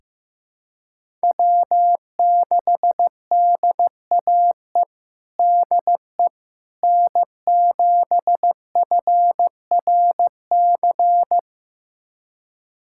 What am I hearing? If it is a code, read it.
Morse code, heard as a keyed tone, about 15 wpm, 700 Hz. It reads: W6DAE DE N7FRC
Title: USA Callsign DE Callsign - 5 Characters - 15wpm